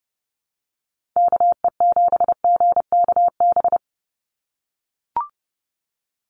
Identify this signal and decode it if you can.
Morse code, heard as a keyed tone, about 30 wpm, 700 Hz. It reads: KE7GKB
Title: USA Call Signs 30wpm